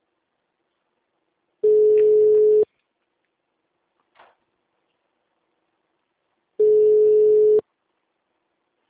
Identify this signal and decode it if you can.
Ringback tone (Europe, 425 Hz).
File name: out- -2008- - - .62725